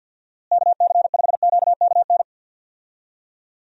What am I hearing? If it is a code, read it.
Morse code, heard as a keyed tone, about 52 wpm, 700 Hz. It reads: KC5QKN
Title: USA Call Signs - 6 Characters - Repeated 3x - 50wpm